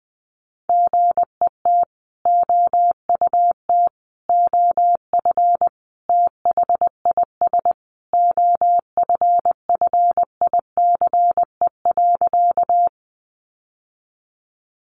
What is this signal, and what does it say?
Morse code, heard as a keyed tone, about 20 wpm, 700 Hz. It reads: GET OUT OF THIS OFFICE.
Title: Sentences from Top 1000 Words 20wpm